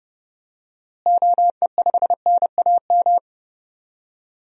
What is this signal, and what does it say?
Morse code, heard as a keyed tone, about 30 wpm, 700 Hz. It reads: OE5NAM
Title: DXCC-100 Call Signs - 6 Characters - Repeated 2x - 30wpm